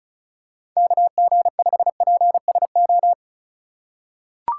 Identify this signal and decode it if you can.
Morse code, heard as a keyed tone, about 34 wpm, 700 Hz. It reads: KG5PSO
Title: USA Call Signs - 3x Character Spacing - 35wpm